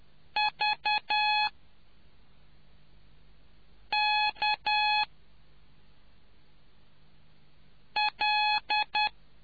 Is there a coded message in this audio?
Yes: Morse code from 0.4 to 9.1 s, heard as a keyed tone, about 10 wpm, 840 Hz. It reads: V K L